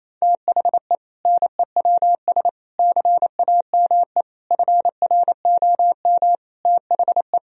Morse code, heard as a keyed tone, about 27 wpm, 700 Hz. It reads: THE NEWS CAME FROM THE
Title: Sentences from Top 1000 Words - Review Part 1 - 28wpm